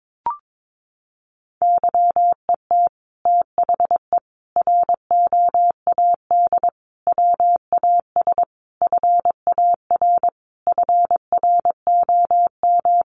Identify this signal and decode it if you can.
Morse code, heard as a keyed tone, about 22 wpm, 700 Hz. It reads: YET THE ROAD WAS FAR FROM
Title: Sentences from Top 1400 Words 22wpm